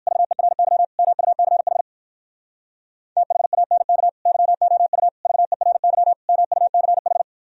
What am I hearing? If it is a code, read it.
Morse code, heard as a keyed tone, about 51 wpm, 700 Hz. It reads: VERY MUCH THANK YOU VERY MUCH
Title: Speed-Racing - Sets of 4 Words 45wpm